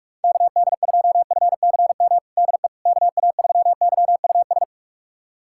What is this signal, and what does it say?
Morse code, heard as a keyed tone, about 41 wpm, 700 Hz. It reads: KD2FCM DE KA3YUS
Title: USA Callsign DE Callsign - 6 Characters - 45wpm